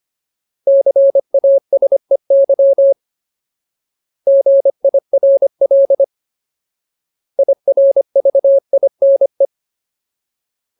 Morse code, heard as a keyed tone, about 25 wpm, 550 Hz. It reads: CASEY GIRL IRVINE
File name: Ultimate Word List / 25WPM Ultimate